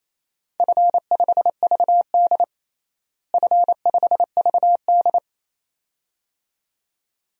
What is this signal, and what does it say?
Morse code, heard as a keyed tone, about 28 wpm, 700 Hz. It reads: F5VD F5VD
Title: DXCC-100 Call Signs - 4 Characters - Repeated 2x - 28wpm